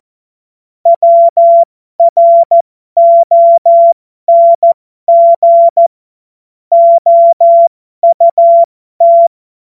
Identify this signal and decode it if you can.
Morse code, heard as a keyed tone, about 14 wpm, spaced as for 12 wpm, 680 Hz. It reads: WRONG OUT